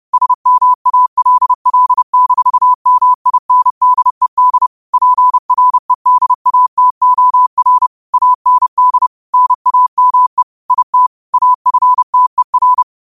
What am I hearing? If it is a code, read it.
Morse code, heard as a keyed tone, about 30 wpm, 1000 Hz. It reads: SMALL-MINDED PREDATOR AND NAME IT AFTER
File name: qotdfiles / 30wpm